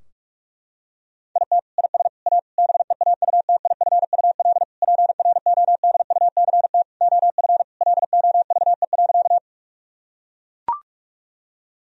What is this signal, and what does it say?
Morse code, heard as a keyed tone, about 41 wpm, 700 Hz. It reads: IT IS A BEAUTIFUL PRODUCT OF LOVE.
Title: Sentences from Top 1000 Words - Review Part 2 - 45wpm